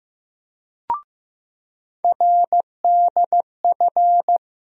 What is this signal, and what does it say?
Morse code, heard as a keyed tone, about 15 wpm, 700 Hz. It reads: RDF
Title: USA Call Sign Postfixes - 3 Letters 15wpm